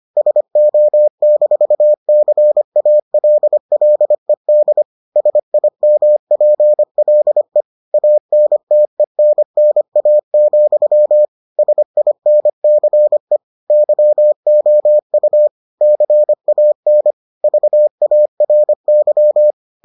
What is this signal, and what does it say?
Morse code, heard as a keyed tone, about 25 wpm, 600 Hz. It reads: SO-CALLED SIMPLE ANTENNA, SINCE YOU CAN VARY